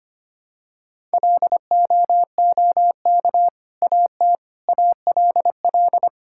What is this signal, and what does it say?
Morse code, heard as a keyed tone, about 25 wpm, 700 Hz. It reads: LOOK AT ALL
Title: Sentences from Top 1000 Words - Review Part 5 - 25wpm